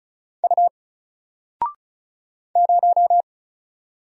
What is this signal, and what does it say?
Morse code, heard as a keyed tone, about 36 wpm, 700 Hz. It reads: U 0
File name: Single-Letter-Number-35wpm